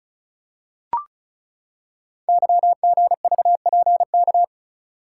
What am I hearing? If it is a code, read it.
Morse code, heard as a keyed tone, about 36 wpm, 700 Hz. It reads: YGVPK
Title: Code Groups - 5 Letters - 35wpm